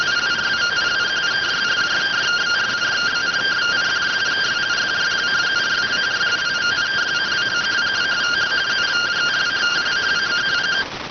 Two-tone teleprinter signal, about 170 Hz shift.